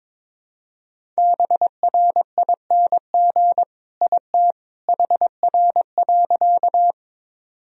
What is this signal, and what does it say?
Morse code, heard as a keyed tone, about 22 wpm, 700 Hz. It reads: BRING IT HR.
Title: Sentences from Top 500 Words and Abbreviations 22wpm